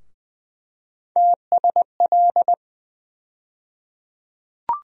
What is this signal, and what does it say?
Morse code, heard as a keyed tone, about 20 wpm, 700 Hz. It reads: TSL
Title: USA Call Sign Postfixes - 3 Letters 20wpm